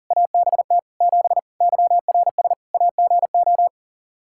Morse code, heard as a keyed tone, about 37 wpm, 700 Hz. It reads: ABT 7 YRS AGO
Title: Practice QSO - Exchange 3 - More Info - 40wpm